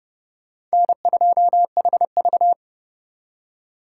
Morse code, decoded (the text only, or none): N2HV